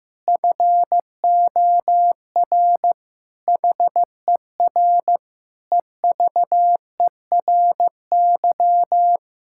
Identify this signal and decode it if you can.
Morse code, heard as a keyed tone, about 16 wpm, 700 Hz. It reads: FOR HER EVERY
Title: Sentences from Top 1000 Words - Review Part 5 - 15wpm